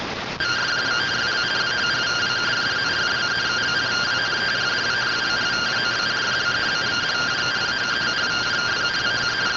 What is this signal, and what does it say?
Two-tone teleprinter signal, about 170 Hz shift.